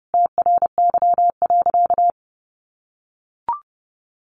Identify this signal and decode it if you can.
Morse code, heard as a keyed tone, about 30 wpm, 700 Hz. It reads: TRY.
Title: Sentences from Top 1300 Words 30wpm